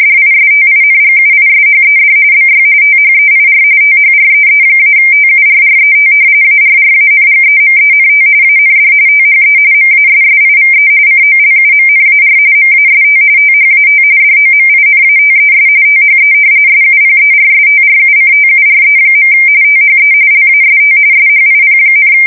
RTTY, about 45.45 baud. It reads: DISTINGUISHED BY THEIR THREE-TURRET ARRANGEMENT AND THE MASSIVE COLUMNAR MAST THAT DOMINATES THE SUPERSTRUCTURE. THE LAST SHIP,
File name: fsk170 / battleships